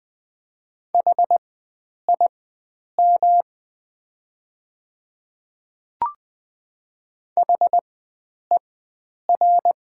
Morse code, heard as a keyed tone, about 20 wpm, spaced as for 9 wpm, 700 Hz. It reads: HIM HER
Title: 4x Character Spacing - Top 100 Words - 20wpm